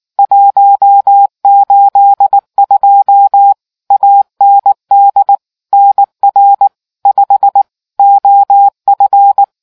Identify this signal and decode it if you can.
Morse code, heard as a keyed tone, about 19 wpm, 800 Hz. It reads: 182 AND NR 5 OF